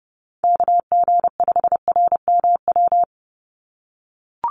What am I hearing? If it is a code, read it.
Morse code, heard as a keyed tone, about 30 wpm, 700 Hz. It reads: KG5RMW
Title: USA Call Signs 30wpm